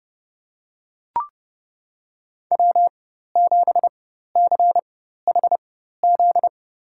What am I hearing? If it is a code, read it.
Morse code, heard as a keyed tone, about 32 wpm, spaced as for 14 wpm, 700 Hz. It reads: W7CHZ
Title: USA Call Signs - 5 Characters - 4x Character Spacing - 30wpm